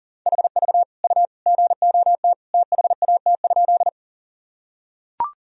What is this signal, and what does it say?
Morse code, heard as a keyed tone, about 40 wpm, 700 Hz. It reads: HV U GOT THAT?
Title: Sentences from Top 300 Words and Abbreviations 40wpm